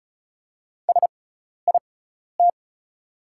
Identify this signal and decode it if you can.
Morse code, heard as a keyed tone, about 36 wpm, spaced as for 12 wpm, 690 Hz. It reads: SIT
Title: 6x Character Spacing - Top 500 Words - 35wpm